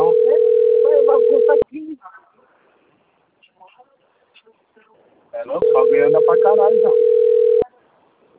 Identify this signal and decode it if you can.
Ringback tone (North America), 440 + 480 Hz, with speech.
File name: rg-1600-unknown- - - .37714